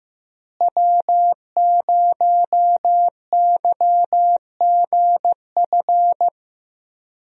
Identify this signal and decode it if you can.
Morse code, heard as a keyed tone, about 15 wpm, 700 Hz. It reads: W0YGF
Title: USA Call Signs - 5 Characters - Repeated 2x - 15wpm